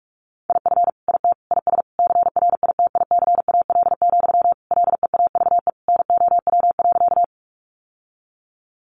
Morse code, heard as a keyed tone, about 44 wpm, 700 Hz. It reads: IF IT IS CRITICAL, LEAVE NOW.
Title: Sentences from Top 1100 Words 45wpm